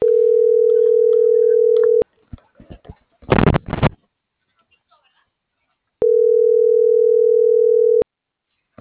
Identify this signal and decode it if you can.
Ringback tone (North America), 440 + 480 Hz.